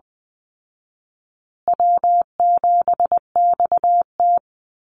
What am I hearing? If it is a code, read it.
Morse code, heard as a keyed tone, about 20 wpm, 700 Hz. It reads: W7XT
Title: Easy Call Signs Repeated 2x - 20wpm